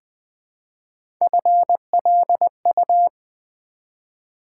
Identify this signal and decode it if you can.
Morse code, heard as a keyed tone, about 20 wpm, 700 Hz. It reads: FLU
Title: Build A Word - 1 Letter At A Time - 3-Letter Words - 20wpm